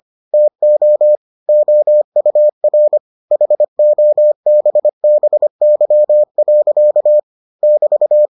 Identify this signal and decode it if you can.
Morse code, heard as a keyed tone, about 25 wpm, 600 Hz. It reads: TO OUR HOBBY. <BT>